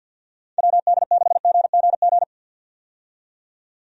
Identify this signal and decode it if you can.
Morse code, heard as a keyed tone, about 51 wpm, 700 Hz. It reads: WD6GGG
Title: USA Call Signs - 6 Characters - 50wpm